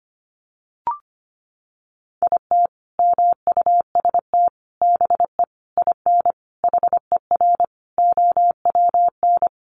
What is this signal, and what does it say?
Morse code, heard as a keyed tone, about 25 wpm, 700 Hz. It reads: IT MUST BE IN HER OWN
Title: Sentences from Top 300 Words 25wpm